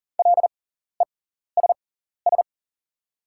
Morse code, heard as a keyed tone, about 41 wpm, spaced as for 14 wpm, 700 Hz. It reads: LESS
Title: 6x Character Spacing - Top 500 Words - 40wpm